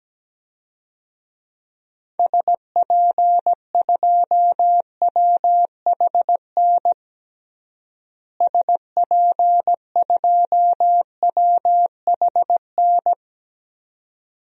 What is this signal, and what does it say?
Morse code, heard as a keyed tone, about 17 wpm, 700 Hz. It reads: SP2WHN SP2WHN
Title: DXCC-100 Call Signs - 6 Characters - Repeated 2x - 17wpm